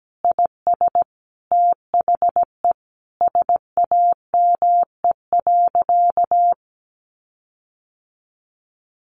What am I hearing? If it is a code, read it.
Morse code, heard as a keyed tone, about 17 wpm, 700 Hz. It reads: IS THE SAME.
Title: Sentences from Top 1400 Words 17wpm